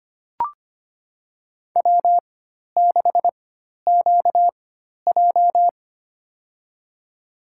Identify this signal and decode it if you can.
Morse code, heard as a keyed tone, about 25 wpm, spaced as for 12 wpm, 700 Hz. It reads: W6QJ
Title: USA Call Signs - 4 Characters - 4x Character Spacing - 25wpm